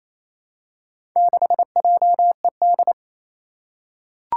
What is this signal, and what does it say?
Morse code, heard as a keyed tone, about 28 wpm, 700 Hz. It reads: 6JED
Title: Four Challenging Letter-Number Combinations 28wpm